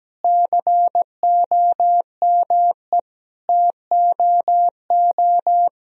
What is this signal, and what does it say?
Morse code, heard as a keyed tone, about 17 wpm, 700 Hz. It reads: COME TOO
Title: Sentences from Top 1000 Words - Review Part 2 - 17wpm